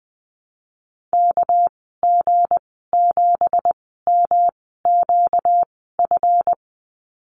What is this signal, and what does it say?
Morse code, heard as a keyed tone, about 20 wpm, spaced as for 15 wpm, 700 Hz. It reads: KG7MQF
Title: 238 Lesson 38g - Letter q - Call Signs 20wpm